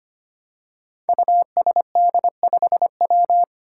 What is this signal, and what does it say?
Morse code, heard as a keyed tone, about 25 wpm, 700 Hz. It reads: USD5W
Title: Five Challenging Letter-Number Combinations 25wpm